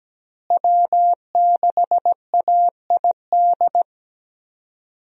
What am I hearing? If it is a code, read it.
Morse code, heard as a keyed tone, about 17 wpm, 700 Hz. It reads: W6AID
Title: USA Call Signs - 5 Characters - Repeated 3x - 17wpm